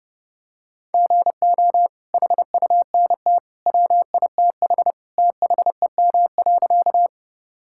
Morse code, heard as a keyed tone, about 30 wpm, 700 Hz. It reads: GO HUNT WITH THEM.
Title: Sentences from Top 1600 Words 30wpm